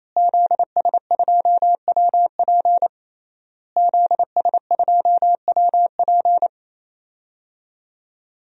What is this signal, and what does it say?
Morse code, heard as a keyed tone, about 28 wpm, 700 Hz. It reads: ZS2WP ZS2WP
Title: DXCC-100 Call Signs - 5 Characters - Repeated 3x - 28wpm